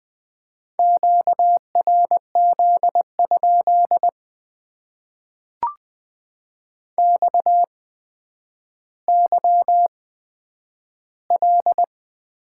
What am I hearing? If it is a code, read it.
Morse code, heard as a keyed tone, about 20 wpm, spaced as for 10 wpm, 700 Hz. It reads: QRZ? XYL
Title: Top 90 QSO Elements - 8x Character Spacing - 20wpm